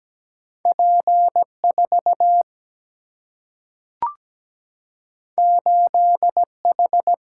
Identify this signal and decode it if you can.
Morse code, heard as a keyed tone, about 17 wpm, 700 Hz. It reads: P4 8H
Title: Single-Letter Single-Number 17wpm